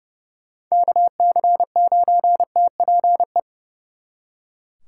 Morse code, heard as a keyed tone, about 39 wpm, 700 Hz. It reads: KC9TPE